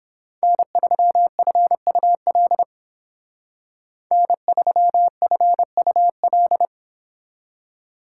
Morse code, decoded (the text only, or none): N3FUL N3FUL